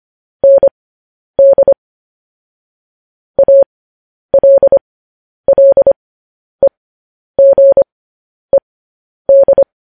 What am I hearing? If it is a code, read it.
Morse code, heard as a keyed tone, about 25 wpm, spaced as for 10 wpm, 550 Hz. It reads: ND ALLEGED
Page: news headlines in morse code at 10 wpm